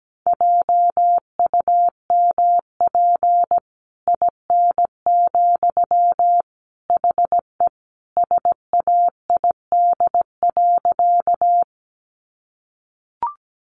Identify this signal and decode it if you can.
Morse code, heard as a keyed tone, about 17 wpm, 700 Hz. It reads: JUMP IN, HE SAID.
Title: Sentences from Top 1300 Words 17wpm